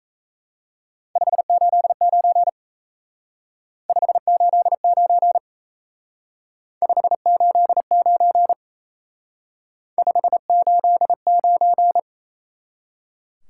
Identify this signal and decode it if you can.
Morse code, heard as a keyed tone, about 35 wpm, 700 Hz. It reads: 589 589 589 589
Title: Speed-Racing Top 180 QSO Elements 28wpm